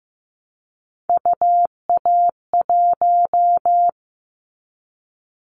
Morse code, heard as a keyed tone, about 15 wpm, 700 Hz. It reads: UA1
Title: Callsign Prefixes for Top 25 DXCC - Repeated 3x - 15wpm